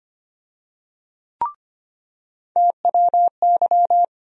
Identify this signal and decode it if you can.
Morse code, heard as a keyed tone, about 25 wpm, 700 Hz. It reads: TWY
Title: USA Call Sign Postfixes - 3 Letters 25wpm